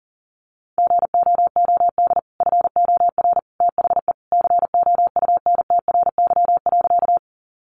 Morse code, heard as a keyed tone, about 40 wpm, 700 Hz. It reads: GOOD FOR THE COUNTRY.
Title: Sentences from Top 300 Words 40wpm